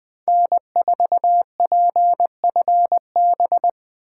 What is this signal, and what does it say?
Morse code, heard as a keyed tone, about 24 wpm, 700 Hz. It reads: N4PFB
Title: Speed Racing - USA Call Signs 20wpm